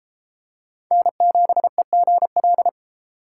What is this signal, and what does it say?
Morse code, heard as a keyed tone, about 33 wpm, 700 Hz. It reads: N7EGL